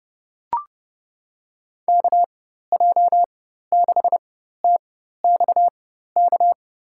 Morse code, heard as a keyed tone, about 30 wpm, spaced as for 14 wpm, 700 Hz. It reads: KJ6TXK
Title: USA Call Signs - 6 Characters - 4x Character Spacing - 30wpm